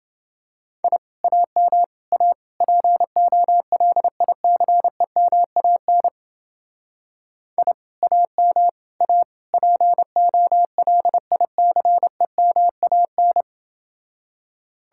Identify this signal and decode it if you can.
Morse code, heard as a keyed tone, about 28 wpm, 700 Hz. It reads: I AM A POLICEMAN I AM A POLICEMAN